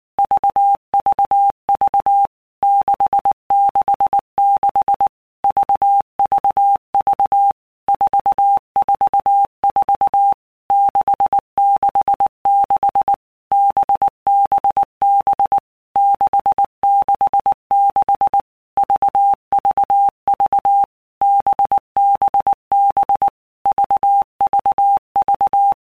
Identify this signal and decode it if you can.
Morse code, heard as a keyed tone, about 19 wpm, 800 Hz. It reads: VVV 666 VVV 444 666 BBB 666 VVV BBB VVV
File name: Basic Letter Groups By Blocks / Groups@18 / VB46@18